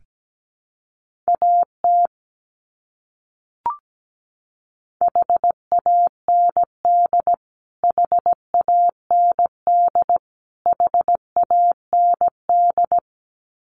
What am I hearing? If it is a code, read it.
Morse code, heard as a keyed tone, about 17 wpm, 700 Hz. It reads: AT HAND HAND HAND
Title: Top 300 Words Repeated 17wpm